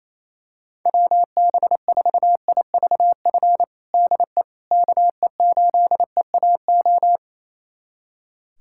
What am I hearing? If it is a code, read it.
Morse code, heard as a keyed tone, about 28 wpm, 700 Hz. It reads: WB4IVF DE KE8EAO
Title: USA Callsign DE Callsign 28wpm